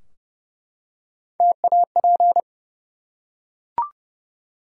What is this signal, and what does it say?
Morse code, heard as a keyed tone, about 28 wpm, 700 Hz. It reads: TAP
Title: Three Letter Words - 30wpm